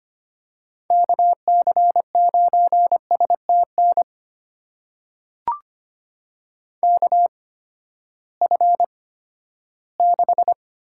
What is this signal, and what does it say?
Morse code, heard as a keyed tone, about 25 wpm, spaced as for 14 wpm, 700 Hz. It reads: KC9STN KF6
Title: USA Call Signs - 6 Characters - 8x Character Spacing - 25wpm